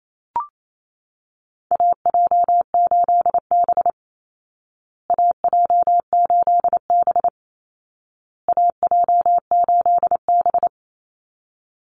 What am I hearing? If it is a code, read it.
Morse code, heard as a keyed tone, about 28 wpm, 700 Hz. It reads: AJ8B AJ8B AJ8B
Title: Call Signs Repeated 2x - 28wpm